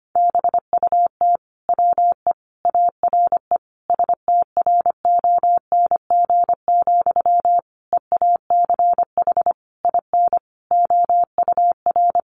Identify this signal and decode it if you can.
Morse code, heard as a keyed tone, about 25 wpm, 700 Hz. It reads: BUT WE ARE STRONG, EACH IN OUR
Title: Sentences from Top 800 Words 25wpm